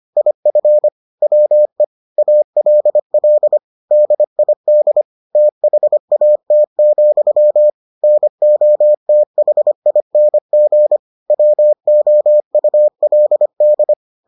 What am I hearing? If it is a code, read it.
Morse code, heard as a keyed tone, about 25 wpm, 600 Hz. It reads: IF WE ALL DID THAT, NOTHING WOULD